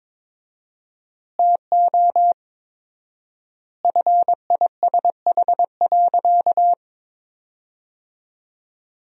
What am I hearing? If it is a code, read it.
Morse code, heard as a keyed tone, about 22 wpm, 700 Hz. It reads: TO FISH.
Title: 4x Word-Spacing - Sentences from Top 300 Words - 22wpm